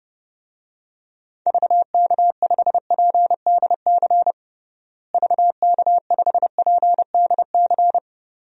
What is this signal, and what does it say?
Morse code, heard as a keyed tone, about 30 wpm, 700 Hz. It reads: VK5PDC VK5PDC
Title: DXCC-100 Call Signs - 6 Characters - Repeated 2x - 30wpm